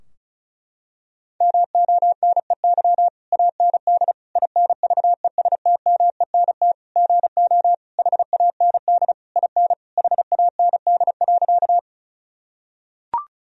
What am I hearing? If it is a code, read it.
Morse code, heard as a keyed tone, about 34 wpm, 700 Hz. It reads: MONEY AND INVESTMENT GO HAND IN HAND.
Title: Sentences from Top 1000 Words - Review Part 4 - 35wpm